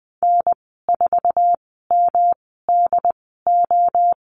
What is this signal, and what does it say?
Morse code, heard as a keyed tone, about 20 wpm, spaced as for 15 wpm, 700 Hz. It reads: N4MDO
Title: 173 Lesson 28g - Number 9 - Call Signs 20wpm